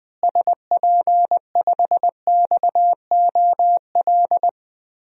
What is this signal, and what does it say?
Morse code, heard as a keyed tone, about 20 wpm, 700 Hz. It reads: SP5XOL
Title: DXCC-100 Call Signs - 6 Characters - Repeated 3x - 20wpm